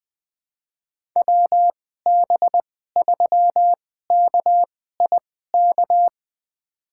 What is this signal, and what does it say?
Morse code, heard as a keyed tone, about 20 wpm, spaced as for 14 wpm, 700 Hz. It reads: WB3KIK
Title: USA Call Signs - 2x Character Spacing - 20wpm